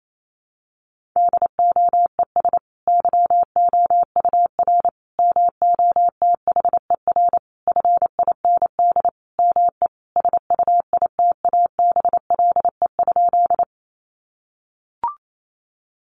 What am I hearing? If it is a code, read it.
Morse code, heard as a keyed tone, about 28 wpm, 700 Hz. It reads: DOES YOUR MOTHER FIND ME SUITABLE?
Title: Sentences from Top 1400 Words 28wpm